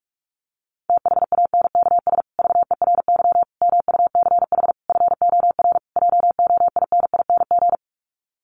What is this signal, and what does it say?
Morse code, heard as a keyed tone, about 44 wpm, 700 Hz. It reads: THANKS VERY MUCH FOR JOINING
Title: Sets of 5 Words Encore2 45wpm